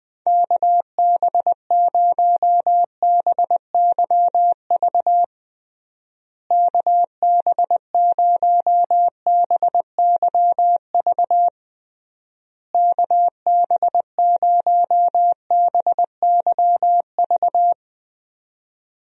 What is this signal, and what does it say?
Morse code, heard as a keyed tone, about 20 wpm, 700 Hz. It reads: KB0BYV KB0BYV KB0BYV
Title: USA Call Signs - 6 Characters - Repeated 3x - 20wpm